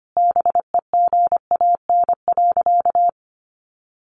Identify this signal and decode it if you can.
Morse code, heard as a keyed tone, about 25 wpm, 700 Hz. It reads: BEGAN.
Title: Sentences from Top 1100 Words 25wpm